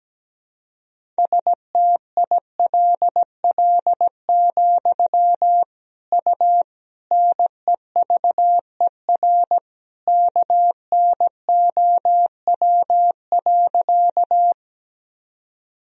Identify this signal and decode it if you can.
Morse code, heard as a keyed tone, about 17 wpm, 700 Hz. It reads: STILL, U NEVER KNOW.
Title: Sentences from Top 300 Words and Abbreviations 17wpm